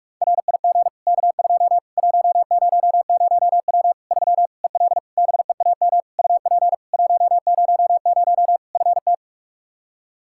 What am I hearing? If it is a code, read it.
Morse code, heard as a keyed tone, about 46 wpm, 700 Hz. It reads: RIG K2 100W 3 EL BEAM UP 100 FT
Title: K1USN SST QSO Practice 45wpm